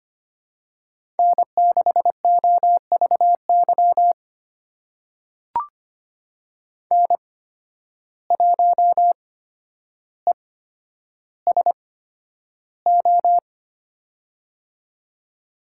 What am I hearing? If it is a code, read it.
Morse code, heard as a keyed tone, about 25 wpm, spaced as for 11 wpm, 700 Hz. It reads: N6OVY N1ESO
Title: USA Call Signs - 5 Characters - 8x Character Spacing - 25wpm